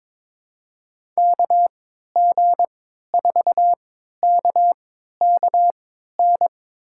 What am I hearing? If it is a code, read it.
Morse code, heard as a keyed tone, about 22 wpm, spaced as for 12 wpm, 700 Hz. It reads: KG4KKN
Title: USA Call Signs - 6 Characters - 3x Character Spacing - 22wpm